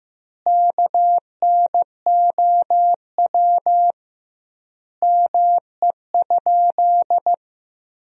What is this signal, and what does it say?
Morse code, heard as a keyed tone, about 15 wpm, 700 Hz. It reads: KNOW ME?
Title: 2x Word-Spacing - Sentences from Top 200 Words - 15wpm